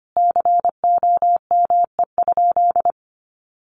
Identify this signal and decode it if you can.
Morse code, heard as a keyed tone, about 25 wpm, 700 Hz. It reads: COME?
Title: Sentences from Top 200 Words 25wpm